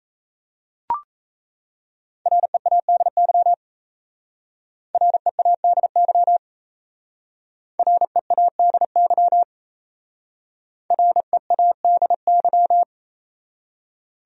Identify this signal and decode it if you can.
Morse code, heard as a keyed tone, about 35 wpm, 700 Hz. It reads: READY READY READY READY
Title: Speed-Racing Top 500 Words 28wpm